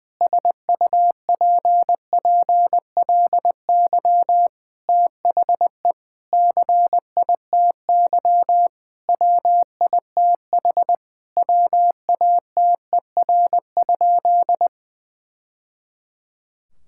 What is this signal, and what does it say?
Morse code, heard as a keyed tone, about 21 wpm, 700 Hz. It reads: SUPPLY THE CITY WITH WATER?
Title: Sentences from Top 1000 Words - Review Part 5 - 20wpm